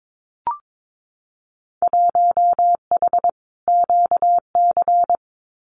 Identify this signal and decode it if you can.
Morse code, heard as a keyed tone, about 22 wpm, 700 Hz. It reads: 1H QC
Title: Winter Field Day Exchanges 22wpm